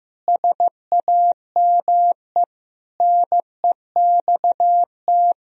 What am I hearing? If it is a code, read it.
Morse code, heard as a keyed tone, about 16 wpm, 700 Hz. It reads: SAME NEXT
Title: Sentences from Top 500 Words and Abbreviations 15wpm